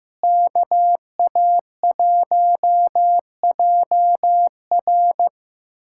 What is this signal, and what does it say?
Morse code, heard as a keyed tone, about 15 wpm, 700 Hz. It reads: KA1JR
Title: USA Callsign DE Callsign - 5 Characters - 15wpm